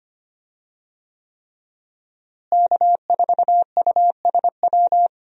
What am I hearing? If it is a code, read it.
Morse code, heard as a keyed tone, about 25 wpm, 700 Hz. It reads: K4USW
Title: USA Call Signs - 5 Characters - Repeated 3x - 25wpm